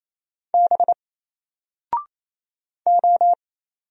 Morse code, heard as a keyed tone, about 27 wpm, 700 Hz. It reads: B O
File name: Single-Letter-Number-28wpm